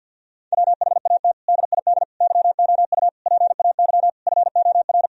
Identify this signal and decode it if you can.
Morse code, heard as a keyed tone, about 50 wpm, 700 Hz. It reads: WHAT DID YOU PAY FOR
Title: Sentences from Top 1000 Words - Review Part 3 - 50wpm